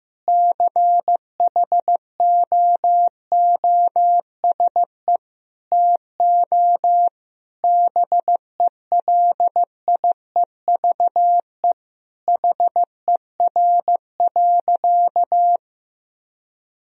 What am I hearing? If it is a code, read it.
Morse code, heard as a keyed tone, about 15 wpm, 700 Hz. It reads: CHOOSE TO BELIEVE HER.
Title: Sentences from Top 1000 Words - Review Part 4 - 15wpm